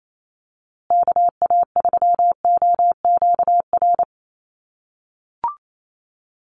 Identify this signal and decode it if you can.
Morse code, heard as a keyed tone, about 28 wpm, 700 Hz. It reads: KA3OQR
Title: USA Call Signs Repeated 3x 28wpm